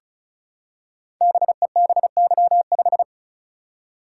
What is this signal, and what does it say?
Morse code, heard as a keyed tone, about 36 wpm, 690 Hz. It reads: BEBY5
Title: Code Groups - 5 Letter:Numbers - 35wpm